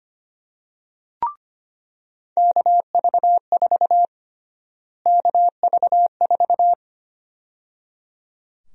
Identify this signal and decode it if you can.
Morse code, heard as a keyed tone, about 25 wpm, 700 Hz. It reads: KV4 KV4
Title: USA Call Sign Prefixes - Repeated 2x - 25wpm